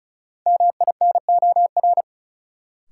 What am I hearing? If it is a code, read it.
Morse code, heard as a keyed tone, about 36 wpm, 700 Hz. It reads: MINOR
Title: Sentences from Top 1500 Words - New Word Drill - 35wpm